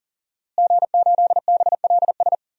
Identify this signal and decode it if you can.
Morse code, heard as a keyed tone, about 40 wpm, 700 Hz. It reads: G8BLS
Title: DXCC-100 Call Signs - 5 Characters - Repeated 2x - 40wpm